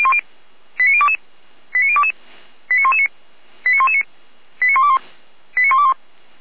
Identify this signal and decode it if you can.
Five-tone selective call (ZVEI), heard as quick runs of five tones, tones of about 70 ms. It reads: 89022 89022 89119 89119 89121 89121